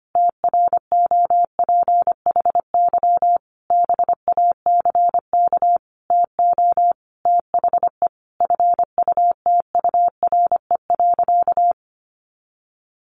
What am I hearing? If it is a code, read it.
Morse code, heard as a keyed tone, about 25 wpm, 700 Hz. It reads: TROPHY BACK TO THE FUTURE.
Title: A Sound Of Thunder 25wpm